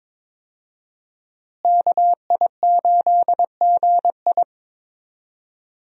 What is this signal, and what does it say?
Morse code, heard as a keyed tone, about 22 wpm, 700 Hz. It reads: KI8GI
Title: USA Call Signs - 5 Characters - Repeated 2x - 22wpm